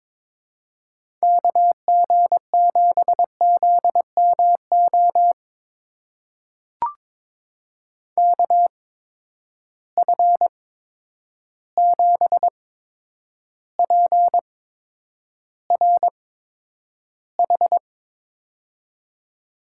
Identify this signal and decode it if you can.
Morse code, heard as a keyed tone, about 22 wpm, spaced as for 9 wpm, 700 Hz. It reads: KG7ZMO KF7PRH
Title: USA Call Signs - 6 Characters - 8x Character Spacing - 22wpm